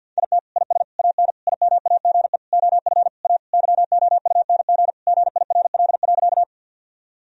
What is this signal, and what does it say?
Morse code, heard as a keyed tone, about 50 wpm, 700 Hz. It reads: IT IS AN IMAGE OF A YOUNG GIRL.
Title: Sentences from Top 1000 Words - Review Part 4 - 50wpm